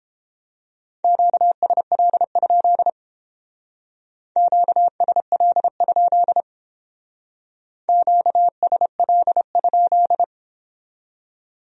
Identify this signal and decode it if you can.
Morse code, heard as a keyed tone, about 31 wpm, 700 Hz. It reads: QSL? QSL? QSL?